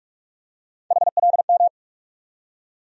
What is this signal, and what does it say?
Morse code, heard as a keyed tone, about 46 wpm, 700 Hz. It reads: HLM
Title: USA Call Sign Postfixes - 3 Letters 45wpm